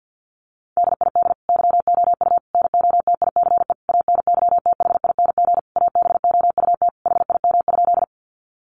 Morse code, heard as a keyed tone, about 50 wpm, 700 Hz. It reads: DID YOU NOTICE ANYTHING ABOUT HIM?
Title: Sentences from Top 1000 Words 50wpm